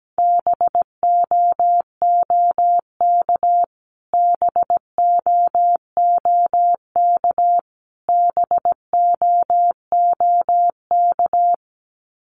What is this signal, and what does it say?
Morse code, heard as a keyed tone, about 17 wpm, 700 Hz. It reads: BOOK BOOK BOOK
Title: Top 500 Words Repeated 17wpm